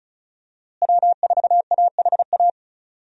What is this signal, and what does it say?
Morse code, heard as a keyed tone, about 34 wpm, 700 Hz. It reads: W4AHA
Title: USA Call Signs - 5 Characters - Repeated 3x - 35wpm